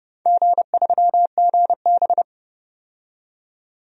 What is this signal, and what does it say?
Morse code, heard as a keyed tone, about 31 wpm, 700 Hz. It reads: G3GB
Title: DXCC-100 Call Signs - 4 Characters - 30wpm